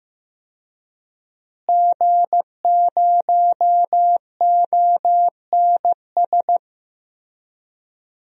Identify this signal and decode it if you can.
Morse code, heard as a keyed tone, about 15 wpm, 700 Hz. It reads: G0ONS
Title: DXCC-100 Call Signs - 5 Characters - Repeated 3x - 15wpm